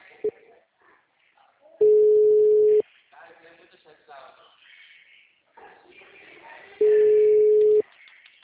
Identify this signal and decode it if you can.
Ringback tone (Europe, 425 Hz), with speech.